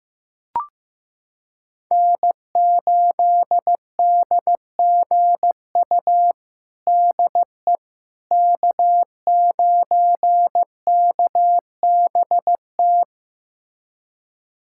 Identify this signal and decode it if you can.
Morse code, heard as a keyed tone, about 16 wpm, 700 Hz. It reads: N8DGU DE K9KBT
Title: USA Callsign DE Callsign - 5 Characters - 15wpm